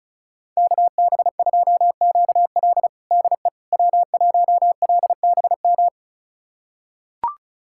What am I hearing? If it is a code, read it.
Morse code, heard as a keyed tone, about 36 wpm, 700 Hz. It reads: KB2QL DE W1LBM
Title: USA Callsign DE Callsign - 5 Characters - 35wpm